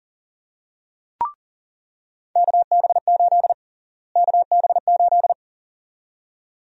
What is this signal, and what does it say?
Morse code, heard as a keyed tone, about 40 wpm, 700 Hz. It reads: KB8 KB8
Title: USA Call Sign Prefixes - Repeated 2x - 40wpm